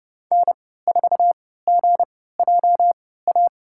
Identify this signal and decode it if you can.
Morse code, heard as a keyed tone, about 30 wpm, spaced as for 17 wpm, 700 Hz. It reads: N4GJA
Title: USA Call Signs - 5 Characters - 3x Character Spacing - 30wpm